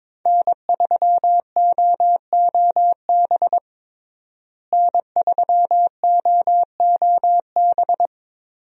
Morse code, heard as a keyed tone, about 22 wpm, 700 Hz. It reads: N3OOB N3OOB
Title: USA Call Signs - 5 Characters - Repeated 3x - 22wpm